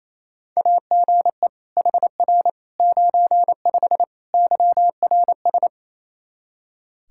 Morse code, heard as a keyed tone, about 27 wpm, 700 Hz. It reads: AGE HR 95 YRS
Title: Practice QSO - Exchange 3 - More Info - 28wpm